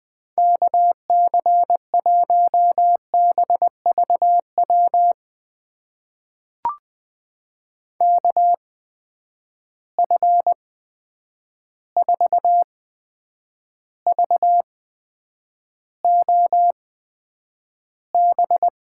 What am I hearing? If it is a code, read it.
Morse code, heard as a keyed tone, about 20 wpm, spaced as for 9 wpm, 700 Hz. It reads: KC1BVW KF4VOB
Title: USA Call Signs - 6 Characters - 8x Character Spacing - 20wpm